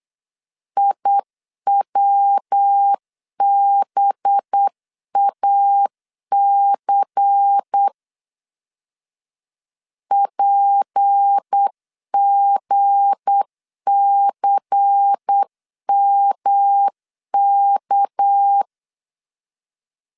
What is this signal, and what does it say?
Morse code, heard as a keyed tone, about 8 wpm, 800 Hz. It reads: IWBAC PGCMK
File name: LEZIONI / 7 GF9Z / 7 G 8wpm 3118